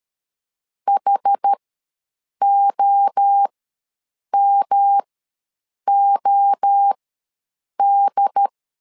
Morse code, heard as a keyed tone, about 13 wpm, spaced as for 7 wpm, 800 Hz. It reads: HOMOD
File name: LEZIONI / 3 MWRO / 3 MWRO ws 12wpm 1914